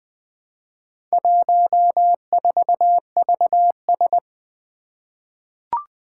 Morse code, heard as a keyed tone, about 20 wpm, 700 Hz. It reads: 14VS
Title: Four Challenging Letter-Number Combinations 20wpm